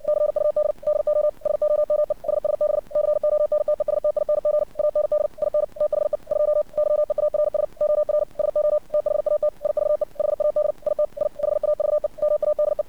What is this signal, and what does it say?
Morse code, heard as a keyed tone, about 60 wpm, 600 Hz. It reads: CUD DO SOME SSB CONTESTING AND IN THE 2 YEARS OR SO THAT IVE HAD IT I HAVE MADE
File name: archived / CRSnet